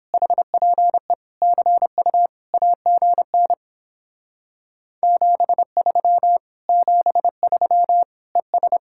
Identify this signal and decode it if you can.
Morse code, heard as a keyed tone, about 31 wpm, 700 Hz. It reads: HPE CU AGN 73 73 ES